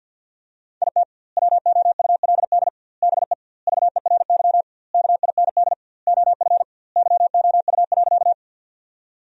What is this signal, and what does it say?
Morse code, heard as a keyed tone, about 50 wpm, 700 Hz. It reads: IT WOULD BE VERY KIND OF YOU.
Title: 2x Word-Spacing - Sentences from Top 200 Words - 50wpm